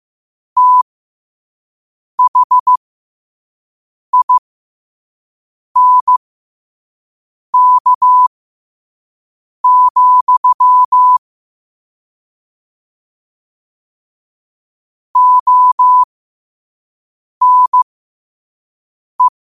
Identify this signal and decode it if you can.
Morse code, heard as a keyed tone, about 15 wpm, spaced as for 5 wpm, 1000 Hz. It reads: THINK, ONE
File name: qotdfiles / 5wpm